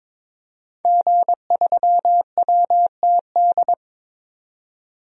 Morse code, heard as a keyed tone, about 22 wpm, 700 Hz. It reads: G3WTD
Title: DXCC-100 Call Signs - 5 Characters - 22wpm